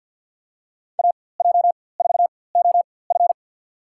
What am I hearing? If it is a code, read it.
Morse code, heard as a keyed tone, about 47 wpm, spaced as for 23 wpm, 700 Hz. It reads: AJ4OF
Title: USA Call Signs - 5 Characters - 4x Character Spacing - 50wpm